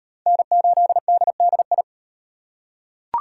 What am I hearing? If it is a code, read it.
Morse code, heard as a keyed tone, about 35 wpm, 700 Hz. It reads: N8DDI